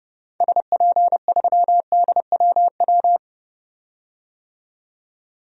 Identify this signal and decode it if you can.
Morse code, heard as a keyed tone, about 30 wpm, 700 Hz. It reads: SP3DWW
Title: DXCC-100 Call Signs - 6 Characters - 30wpm